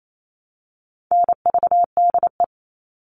Morse code, heard as a keyed tone, about 28 wpm, 700 Hz. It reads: NVDE
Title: All Cut Numbers - 4-Digit 28wpm